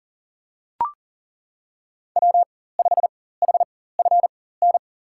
Morse code, heard as a keyed tone, about 40 wpm, spaced as for 19 wpm, 700 Hz. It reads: W5HFN
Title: USA Call Signs - 4x Character Spacing - 40wpm